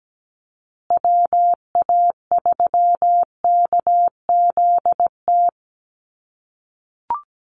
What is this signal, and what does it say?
Morse code, heard as a keyed tone, about 17 wpm, 700 Hz. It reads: WA3KZT
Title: USA Call Signs Repeated 3x 17wpm